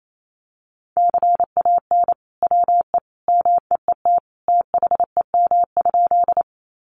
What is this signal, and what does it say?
Morse code, heard as a keyed tone, about 28 wpm, 700 Hz. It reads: CAN WE MEET THEM?
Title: Sentences from Top 1100 Words 28wpm